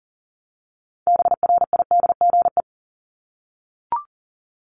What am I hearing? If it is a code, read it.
Morse code, heard as a keyed tone, about 40 wpm, 700 Hz. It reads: BRIDGE
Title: Top 1001-1250 Words 40wpm